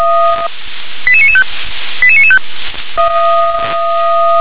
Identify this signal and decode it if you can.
Five-tone selective call (ZVEI), heard as quick runs of five tones, tones of about 70 ms. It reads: 88995 88995